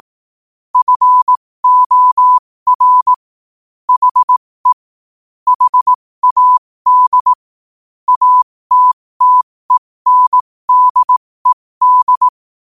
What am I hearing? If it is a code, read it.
Morse code, heard as a keyed tone, about 18 wpm, spaced as for 15 wpm, 1000 Hz. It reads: FOR HE HAD ATTENDED
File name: qotdfiles / 15wpm